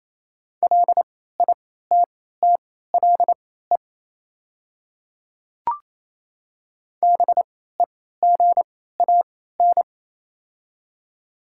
Morse code, heard as a keyed tone, about 27 wpm, spaced as for 16 wpm, 690 Hz. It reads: LITTLE BEGAN
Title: 3x Character Spacing - Top 500 Words - 28wpm